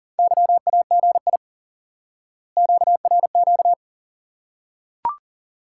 Morse code, heard as a keyed tone, about 39 wpm, 700 Hz. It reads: YAGI QRQ
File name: IQE Course 3 - Lesson 89 - YAGI - Two Elements - 40wpm